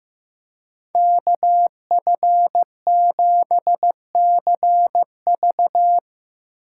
Morse code, heard as a keyed tone, about 15 wpm, 700 Hz. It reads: KF7CV
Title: USA Call Signs - 5 Characters - 2x Character Spacing - 15wpm